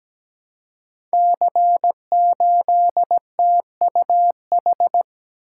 Morse code, heard as a keyed tone, about 17 wpm, 690 Hz. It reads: C8TUH
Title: Code Groups - 5 Letter:Numbers - 17wpm